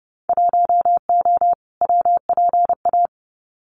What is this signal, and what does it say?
Morse code, heard as a keyed tone, about 30 wpm, 700 Hz. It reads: 1O WPA
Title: Winter Field Day Exchanges 30wpm